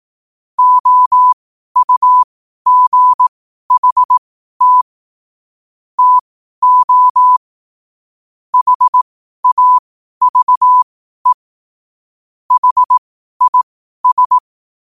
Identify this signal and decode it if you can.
Morse code, heard as a keyed tone, about 18 wpm, spaced as for 12 wpm, 1000 Hz. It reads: OUGHT TO HAVE HIS
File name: qotdfiles / 12wpm